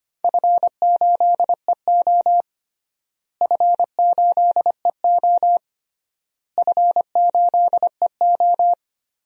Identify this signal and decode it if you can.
Morse code, heard as a keyed tone, about 25 wpm, 700 Hz. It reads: F8EO F8EO F8EO
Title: DXCC-100 Call Signs - 4 Characters - Repeated 3x - 25wpm